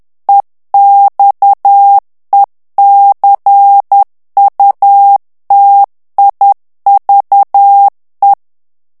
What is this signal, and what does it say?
Morse code, heard as a keyed tone, about 10 wpm, 800 Hz. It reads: EXECUTIVE
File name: ANGLAIS texte clair 3 vitesse 10 mots minute